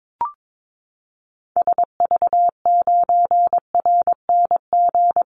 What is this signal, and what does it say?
Morse code, heard as a keyed tone, about 22 wpm, 700 Hz. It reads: SV9RNG
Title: Call Signs Repeated 3x - 22wpm